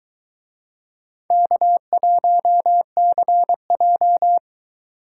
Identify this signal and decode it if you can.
Morse code, heard as a keyed tone, about 22 wpm, 700 Hz. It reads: K1CJ